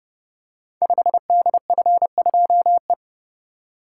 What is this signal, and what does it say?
Morse code, heard as a keyed tone, about 30 wpm, 700 Hz. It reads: 5DF2E
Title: Code Groups - 5 Letter:Numbers - 30wpm